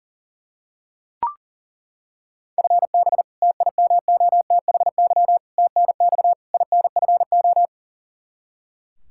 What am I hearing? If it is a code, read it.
Morse code, heard as a keyed tone, about 40 wpm, 690 Hz. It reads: FB TIMOTHY TNX INFO
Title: Practice QSO - Exchange 3 - More Info - 40wpm